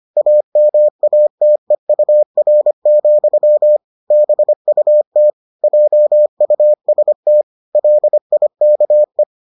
Morse code, heard as a keyed tone, about 25 wpm, 600 Hz. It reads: AMATEUR, BUT JUST LIKE